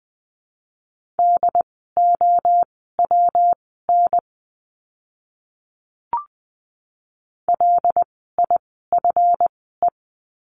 Morse code, heard as a keyed tone, about 20 wpm, spaced as for 15 wpm, 700 Hz. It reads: DOWN LIFE
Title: 237 Lesson 38f - Letter q - Words 20wpm